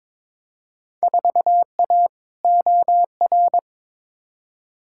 Morse code, heard as a keyed tone, about 22 wpm, 700 Hz. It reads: 4A OR
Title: Field Day Exchanges - 22wpm